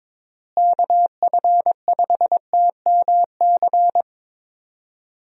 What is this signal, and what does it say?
Morse code, heard as a keyed tone, about 21 wpm, 700 Hz. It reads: KF5TMC
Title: USA Call Signs - 6 Characters - Repeated 3x - 22wpm